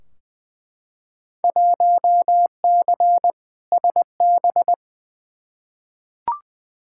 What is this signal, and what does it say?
Morse code, heard as a keyed tone, about 21 wpm, 700 Hz. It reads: 1C SB
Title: Field Day Exchanges Encore - 20wpm